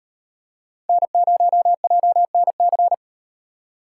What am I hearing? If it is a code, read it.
Morse code, heard as a keyed tone, about 38 wpm, 700 Hz. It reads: N0JNC